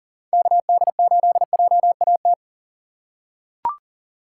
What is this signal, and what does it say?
Morse code, heard as a keyed tone, about 40 wpm, 700 Hz. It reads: KD8JAT
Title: USA Call Signs - 6 Characters - 3x Character Spacing - 40wpm